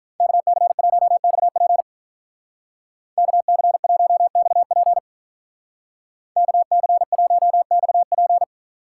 Morse code, heard as a keyed tone, about 42 wpm, 700 Hz. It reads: KC1XP KC1XP KC1XP
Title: Speed Racing - USA 2x2 Call Signs 35wpm